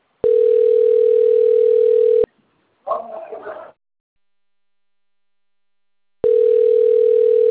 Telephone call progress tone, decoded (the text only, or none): ringback tone